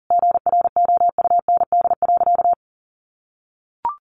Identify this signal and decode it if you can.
Morse code, heard as a keyed tone, about 40 wpm, 700 Hz. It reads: GROUND.
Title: Sentences from Top 500 Words 40wpm